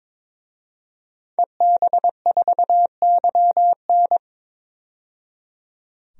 Morse code, heard as a keyed tone, about 21 wpm, 700 Hz. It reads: EB4YN
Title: DXCC-100 Call Signs - 5 Characters - 22wpm